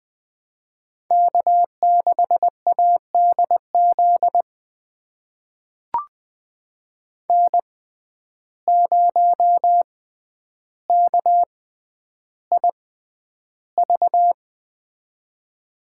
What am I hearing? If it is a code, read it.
Morse code, heard as a keyed tone, about 20 wpm, spaced as for 10 wpm, 700 Hz. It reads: K6ADZ N0KIV
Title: USA Call Signs - 5 Characters - 6x Character Spacing - 20wpm